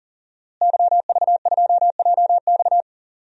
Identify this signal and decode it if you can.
Morse code, heard as a keyed tone, about 37 wpm, 700 Hz. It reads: YV2JX
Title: DXCC-100 Call Signs - 5 Characters - Repeated 3x - 40wpm